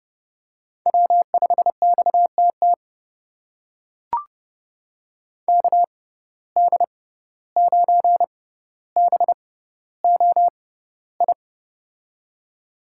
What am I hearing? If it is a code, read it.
Morse code, heard as a keyed tone, about 30 wpm, spaced as for 15 wpm, 700 Hz. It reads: W5XTT KD9BOI
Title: USA Call Signs - 6x Character Spacing - 30wpm